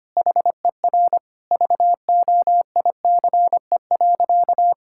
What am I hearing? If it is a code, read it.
Morse code, heard as a keyed tone, about 25 wpm, 700 Hz. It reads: HER VOICE.
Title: Sentences from Top 1000 Words - Review Part 6 - 25wpm